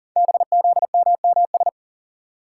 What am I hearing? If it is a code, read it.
Morse code, heard as a keyed tone, about 38 wpm, 690 Hz. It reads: BZMMS
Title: Code Groups - 5 Letters - 40wpm